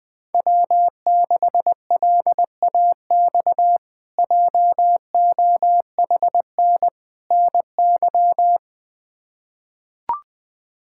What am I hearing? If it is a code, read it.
Morse code, heard as a keyed tone, about 20 wpm, 700 Hz. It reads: W6LAX JOHN NY
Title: K1USN SST QSO Practice - 20wpm